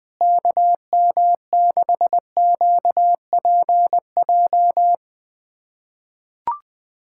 Morse code, heard as a keyed tone, about 21 wpm, 700 Hz. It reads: KM6QPJ
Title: USA Callsign DE Callsign 20wpm